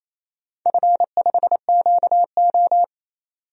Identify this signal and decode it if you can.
Morse code, heard as a keyed tone, about 28 wpm, 700 Hz. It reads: F5QO
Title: DXCC-100 Call Signs - 4 Characters - 28wpm